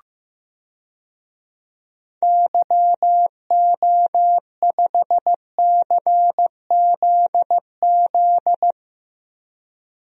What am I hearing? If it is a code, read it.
Morse code, heard as a keyed tone, about 15 wpm, 700 Hz. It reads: YO5CZZ
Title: DXCC-100 Call Signs - 6 Characters - 15wpm